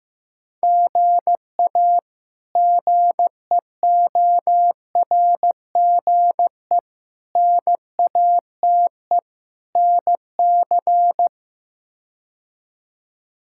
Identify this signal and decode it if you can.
Morse code, heard as a keyed tone, about 15 wpm, 700 Hz. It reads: GA GEORGE NATE NC
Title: K1USN SST QSO Practice - 15wpm